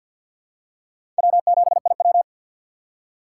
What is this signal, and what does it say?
Morse code, heard as a keyed tone, about 47 wpm, 700 Hz. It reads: W7IW
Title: USA Call Signs - 4 Characters - 3x Character Spacing - 50wpm